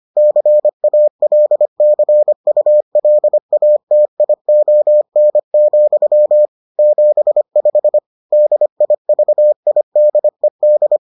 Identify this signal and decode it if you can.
Morse code, heard as a keyed tone, about 25 wpm, 600 Hz. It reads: CALCULATION, 75 DIVIDED